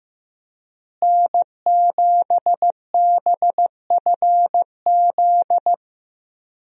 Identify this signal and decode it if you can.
Morse code, heard as a keyed tone, about 15 wpm, 700 Hz. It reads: N7BFZ
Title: USA Call Signs - 5 Characters - Repeated 2x - 15wpm